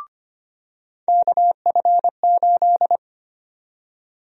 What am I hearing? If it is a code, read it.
Morse code, heard as a keyed tone, about 25 wpm, 700 Hz. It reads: KF8
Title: USA Call Sign Prefixes - 25wpm